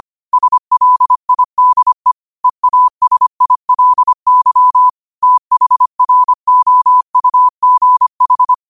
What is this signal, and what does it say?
Morse code, heard as a keyed tone, about 25 wpm, 1000 Hz. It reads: SLIDE EASILY THROUGH
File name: qotdfiles / 25wpm